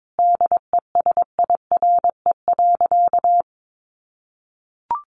Morse code, heard as a keyed tone, about 22 wpm, 700 Hz. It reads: DESIRE.
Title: Sentences from Top 1100 Words 22wpm